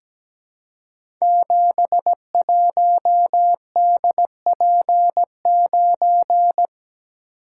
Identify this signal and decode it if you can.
Morse code, heard as a keyed tone, about 17 wpm, 700 Hz. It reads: 71DP9
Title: Code Groups - 5 Letter:Numbers - 17wpm